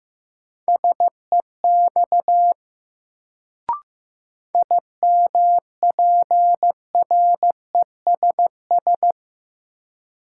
Morse code, heard as a keyed tone, about 15 wpm, 700 Hz. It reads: SEX IMPRESS
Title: Sentences from Top 1600 Words - New Word Drill - 15wpm